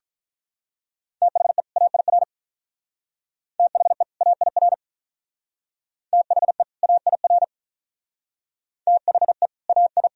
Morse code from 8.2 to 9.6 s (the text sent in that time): THE